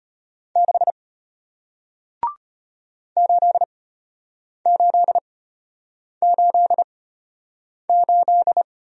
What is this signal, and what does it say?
Morse code, heard as a keyed tone, about 33 wpm, spaced as for 8 wpm, 690 Hz. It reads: 6 8888